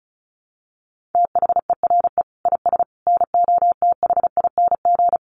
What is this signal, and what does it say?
Morse code, heard as a keyed tone, about 34 wpm, 700 Hz. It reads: THERE IS NOTHING
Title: Sentences from Top 800 Words 35wpm